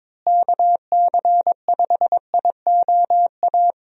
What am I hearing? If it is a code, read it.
Morse code, heard as a keyed tone, about 21 wpm, 700 Hz. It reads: KC5IOA
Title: USA Call Signs - 6 Characters - Repeated 3x - 22wpm